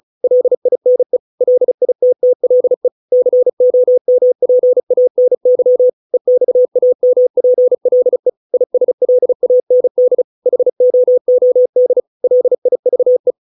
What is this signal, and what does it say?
Morse code, heard as a keyed tone, about 34 wpm, 500 Hz. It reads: LINE LITTLE COMPANY EXAMPLE ISLAND HOOD LIVE